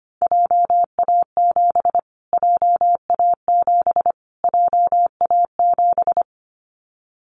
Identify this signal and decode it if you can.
Morse code, heard as a keyed tone, about 25 wpm, 700 Hz. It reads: JA7 JA7 JA7
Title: Callsign Prefixes for Top 25 DXCC - Repeated 3x - 25wpm